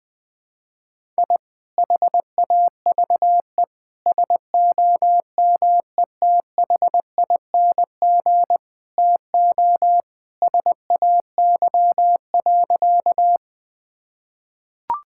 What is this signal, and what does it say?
Morse code, heard as a keyed tone, about 20 wpm, 700 Hz. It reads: I HAVE SOMETHING TO SAY.
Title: Sentences with 2-Word Phrases - Spoken Beginning - 20wpm